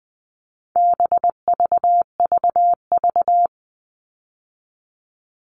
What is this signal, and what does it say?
Morse code, heard as a keyed tone, about 20 wpm, 700 Hz. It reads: BVVV
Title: All Cut Numbers - 4-Digit 20wpm